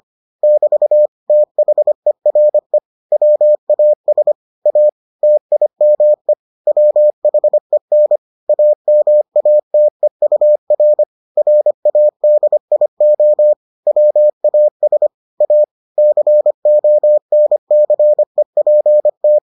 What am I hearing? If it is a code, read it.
Morse code, heard as a keyed tone, about 25 wpm, 600 Hz. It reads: <BT> THERE WAS A TIME WHEN AMATEUR RADIO WAS A CONCEPT